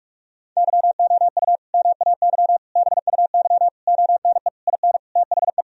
Morse code, heard as a keyed tone, about 46 wpm, 700 Hz. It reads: YOU MAY BUY ONE IN THE
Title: Sentences from Top 1000 Words - Review Part 4 - 45wpm